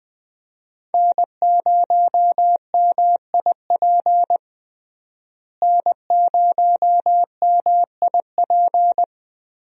Morse code, heard as a keyed tone, about 21 wpm, 700 Hz. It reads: N0MIP N0MIP
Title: USA Call Signs - 5 Characters - Repeated 2x - 20wpm